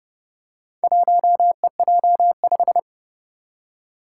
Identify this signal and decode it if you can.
Morse code, heard as a keyed tone, about 30 wpm, 700 Hz. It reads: 1EJ5
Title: Four Challenging Letter-Number Combinations 30wpm